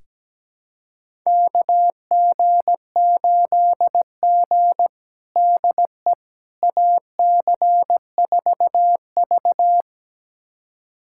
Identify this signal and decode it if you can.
Morse code, heard as a keyed tone, about 17 wpm, 700 Hz. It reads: KG8G DE AC4V
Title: USA Callsign DE Callsign - 4 Characters - 17wpm